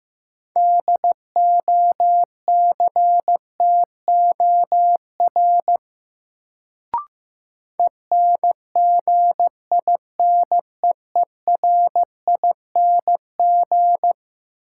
Morse code, heard as a keyed tone, about 15 wpm, 700 Hz. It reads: DOCTOR ENGINEERING
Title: Sentences from Top 1500 Words - New Word Drill - 15wpm